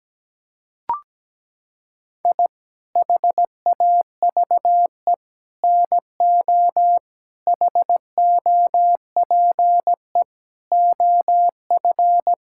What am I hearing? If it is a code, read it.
Morse code, heard as a keyed tone, about 17 wpm, 700 Hz. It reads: I HAVE NO HOPE OF
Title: Sentences from Top 1000 Words - Review Part 5 - 17wpm